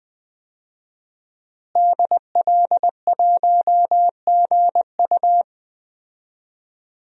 Morse code, heard as a keyed tone, about 20 wpm, 700 Hz. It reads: DL1GU
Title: DXCC-100 Call Signs - 5 Characters - Repeated 2x - 20wpm